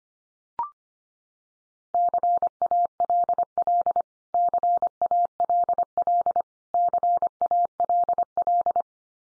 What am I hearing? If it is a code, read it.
Morse code, heard as a keyed tone, about 25 wpm, 700 Hz. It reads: CALL CALL CALL
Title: Top 100 Words 25wpm x3